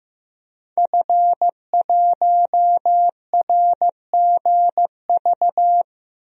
Morse code, heard as a keyed tone, about 16 wpm, 700 Hz. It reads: F1RGV
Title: DXCC-100 Call Signs - 5 Characters - 15wpm